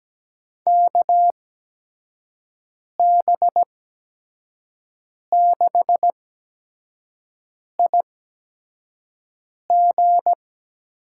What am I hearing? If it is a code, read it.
Morse code, heard as a keyed tone, about 17 wpm, spaced as for 5 wpm, 700 Hz. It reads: KB6IG
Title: USA Call Signs - 6 Characters - 8x Character Spacing - 17wpm